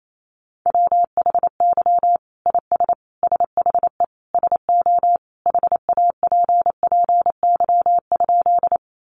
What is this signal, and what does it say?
Morse code, heard as a keyed tone, about 28 wpm, 700 Hz. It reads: WHY IS SHE SO HAPPY?
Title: Sentences from Top 1000 Words 28wpm